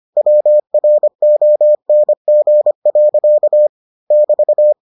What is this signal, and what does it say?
Morse code, heard as a keyed tone, about 25 wpm, 600 Hz. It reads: WRONG. <BT>